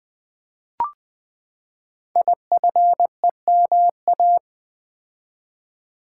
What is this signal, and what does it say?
Morse code, heard as a keyed tone, about 20 wpm, 700 Hz. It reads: IFEMA
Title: Code Groups - 5 Letter:Numbers - 20wpm